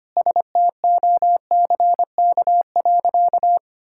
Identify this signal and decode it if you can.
Morse code, heard as a keyed tone, about 25 wpm, 700 Hz. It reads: STOCK.
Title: Sentences from Top 1000 Words - Review Part 5 - 25wpm